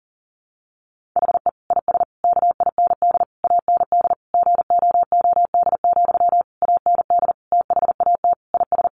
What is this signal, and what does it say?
Morse code, heard as a keyed tone, about 40 wpm, 700 Hz. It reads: HE IS KIND AND GOOD, AND THAT IS
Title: Sentences from Top 1400 Words 40wpm